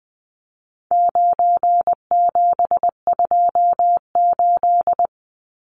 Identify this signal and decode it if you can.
Morse code, heard as a keyed tone, about 20 wpm, 700 Hz. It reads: 9728
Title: Numbers - 4-Digit 20wpm